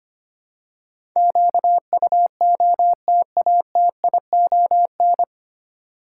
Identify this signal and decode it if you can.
Morse code, heard as a keyed tone, about 25 wpm, 700 Hz. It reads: QUOTATION
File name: practice / 25wpm / 9 Letter Words - First 3 Letters Given - 25wpm